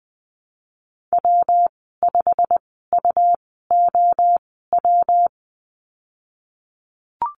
Morse code, heard as a keyed tone, about 20 wpm, spaced as for 15 wpm, 700 Hz. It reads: W5UOW
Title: 186 Lesson 30g - Letter b - Call Signs 20wpm